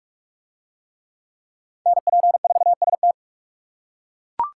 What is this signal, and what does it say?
Morse code, heard as a keyed tone, about 39 wpm, 700 Hz. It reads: NP4ST